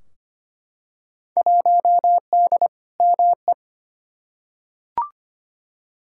Morse code, heard as a keyed tone, about 25 wpm, 700 Hz. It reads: 1D ME
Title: Field Day Exchanges - 25wpm